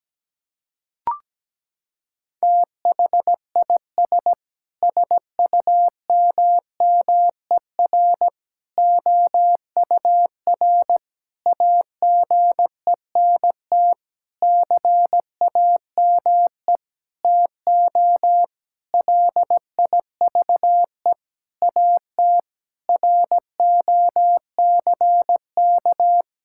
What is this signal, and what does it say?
Morse code, heard as a keyed tone, about 17 wpm, 700 Hz. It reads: THIS SUMMER OUR AGENT CAME TO LIVE AT ROCK
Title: Sentences from Top 1600 Words 17wpm